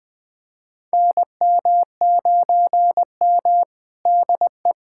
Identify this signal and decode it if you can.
Morse code, heard as a keyed tone, about 20 wpm, 700 Hz. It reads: NM9M DE
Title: USA Callsign DE Callsign - 4 Characters - 20wpm